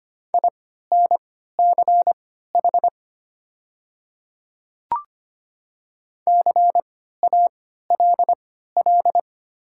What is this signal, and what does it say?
Morse code, heard as a keyed tone, about 25 wpm, spaced as for 14 wpm, 700 Hz. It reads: INCH CALL
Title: 3x Character Spacing - Top 500 Words - 25wpm